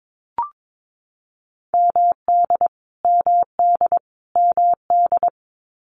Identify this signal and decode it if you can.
Morse code, heard as a keyed tone, about 22 wpm, 700 Hz. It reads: MD MD MD
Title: US States Repeated 3x 22wpm